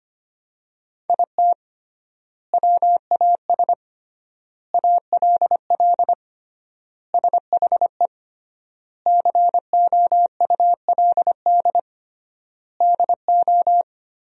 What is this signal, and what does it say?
Morse code, heard as a keyed tone, about 25 wpm, 700 Hz. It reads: IT WAS ALL SHE COULD DO
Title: 3x Word-Spacing - Sentences from Top 400 Words - 25wpm